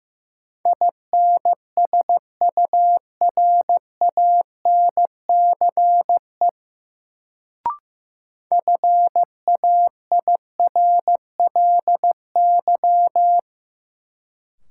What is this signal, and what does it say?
Morse code, heard as a keyed tone, about 15 wpm, 700 Hz. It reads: INSURANCE FAIRLY
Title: Sentences from Top 1100 Words - New Word Drill Alt - 15wpm